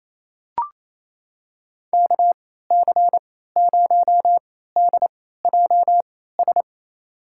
Morse code, heard as a keyed tone, about 28 wpm, spaced as for 16 wpm, 700 Hz. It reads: KC0DJS
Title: USA Call Signs - 6 Characters - 3x Character Spacing - 28wpm